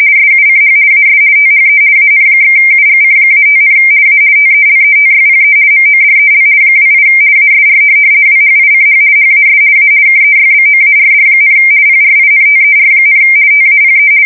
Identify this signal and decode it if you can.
RTTY, about 50 baud. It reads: SUCH AS THE AN/UGC-5, A TELETYPE MODEL 28 ASR WHICH HAS A KEYBOARD, PRINTER, TAPE
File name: fsk170-50B / teletype-corp